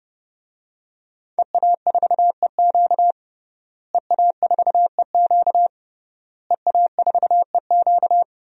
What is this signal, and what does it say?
Morse code, heard as a keyed tone, about 30 wpm, 700 Hz. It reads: EA4EQ EA4EQ EA4EQ
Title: DXCC-100 Call Signs - 5 Characters - Repeated 3x - 30wpm